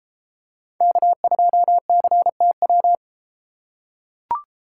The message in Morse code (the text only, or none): K2CTW